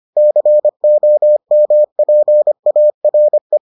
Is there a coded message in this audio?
Yes: Morse code from 0.2 to 3.6 s, heard as a keyed tone, about 25 wpm, 600 Hz. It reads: COMPARE